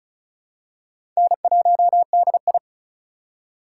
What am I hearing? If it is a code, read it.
Morse code, heard as a keyed tone, about 36 wpm, 700 Hz. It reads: N1DI
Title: USA Call Signs - 4 Characters - 4x Character Spacing - 35wpm